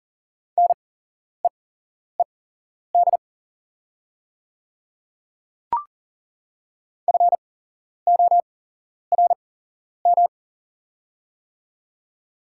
Morse code, heard as a keyed tone, about 37 wpm, spaced as for 11 wpm, 700 Hz. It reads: NEED FORM
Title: 8x Character Spacing - Top 200 Words - 40wpm